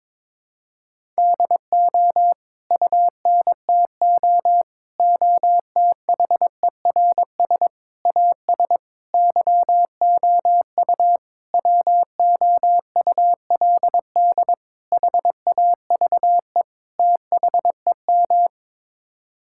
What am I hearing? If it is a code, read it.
Morse code, heard as a keyed tone, about 22 wpm, 700 Hz. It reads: DO UNTO OTHERS AS YOU WOULD HAVE THEM
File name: practice / 22wpm / Sets of 8 Words - 22wpm